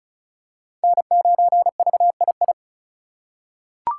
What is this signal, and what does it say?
Morse code, heard as a keyed tone, about 36 wpm, 700 Hz. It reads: N9VII
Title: USA Call Signs - 2x Character Spacing - 35wpm